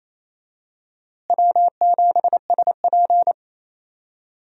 Morse code, heard as a keyed tone, about 28 wpm, 700 Hz. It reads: W7SP
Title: USA Call Signs - 4 Characters - Repeated 3x - 28wpm